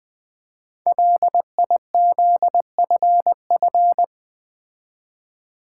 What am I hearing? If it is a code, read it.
Morse code, heard as a keyed tone, about 20 wpm, 700 Hz. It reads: LIZFF
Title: Code Groups - 5 Letters - 20wpm